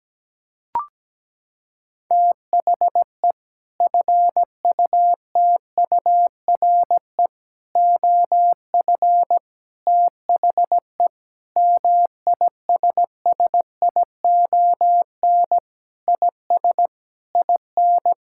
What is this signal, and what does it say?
Morse code, heard as a keyed tone, about 17 wpm, 700 Hz. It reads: THE FUTURE OF THE MISSION IS IN
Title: Sentences from Top 1600 Words 17wpm